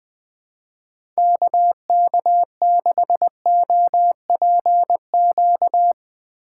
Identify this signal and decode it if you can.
Morse code, heard as a keyed tone, about 20 wpm, 700 Hz. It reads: KK6OPQ
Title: USA Call Signs - 6 Characters - 6x Character Spacing - 20wpm